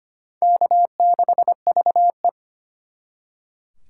Morse code, heard as a keyed tone, about 32 wpm, 700 Hz. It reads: K6VE